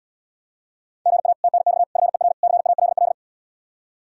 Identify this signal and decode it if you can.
Morse code, heard as a keyed tone, about 80 wpm, 700 Hz. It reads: GA TNX FER CALL